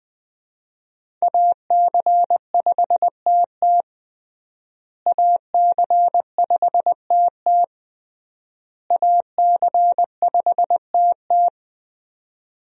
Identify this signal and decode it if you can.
Morse code, heard as a keyed tone, about 20 wpm, 700 Hz. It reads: AC5TT AC5TT AC5TT
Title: USA Call Signs - 5 Characters - Repeated 3x - 20wpm